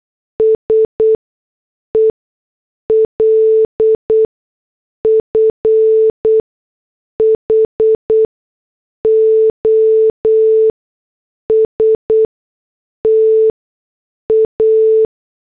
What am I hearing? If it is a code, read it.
Morse code, heard as a keyed tone, about 8 wpm, spaced as for 6 wpm, 440 Hz. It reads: SELF HOSTA